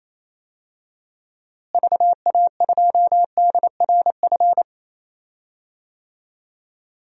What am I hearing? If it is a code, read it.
Morse code, heard as a keyed tone, about 28 wpm, 700 Hz. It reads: VA2DRF
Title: DXCC-100 Call Signs - 6 Characters - Repeated 3x - 28wpm